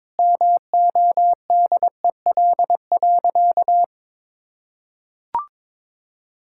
Morse code, heard as a keyed tone, about 21 wpm, 700 Hz. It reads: MODEL.
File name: Sentences with 7-Word Phrases - 22wpm